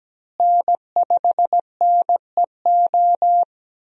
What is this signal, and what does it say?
Morse code, heard as a keyed tone, about 17 wpm, 700 Hz. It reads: N5NEO
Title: USA Call Signs - 5 Characters - Repeated 2x - 17wpm